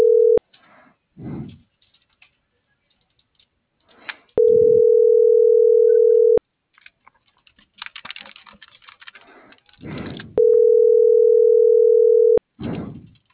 Ringback tone (North America), 440 + 480 Hz.